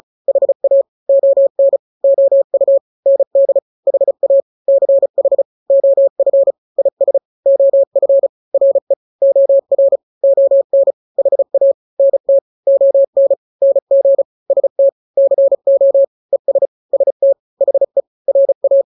Morse code, heard as a keyed tone, about 35 wpm, 550 Hz. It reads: HA ON OU ND HA CH OF IS OF RE OR ON HA NT ON NG ST CO ES ST HE RA
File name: n-grams / 2-grams 35 30 WPM